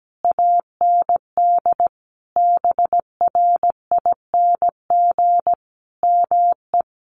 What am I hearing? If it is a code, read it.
Morse code, heard as a keyed tone, about 17 wpm, 700 Hz. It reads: AND BRING ME
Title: Sentences from Top 1200 Words 17wpm